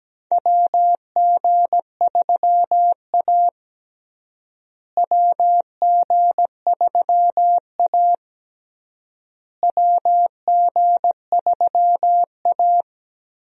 Morse code, heard as a keyed tone, about 17 wpm, 700 Hz. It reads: WG3A WG3A WG3A
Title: USA Call Signs - 4 Characters - Repeated 3x - 17wpm